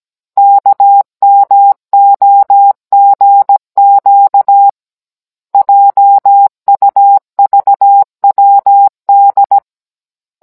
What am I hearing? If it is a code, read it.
Morse code, heard as a keyed tone, about 17 wpm, 800 Hz. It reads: KMOGQ JUVWD
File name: Cours-NR19-16mm